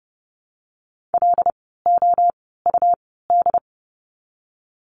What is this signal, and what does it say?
Morse code, heard as a keyed tone, about 30 wpm, spaced as for 17 wpm, 700 Hz. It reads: LOUD
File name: Lesson 17f - Letter u - Words 30wpm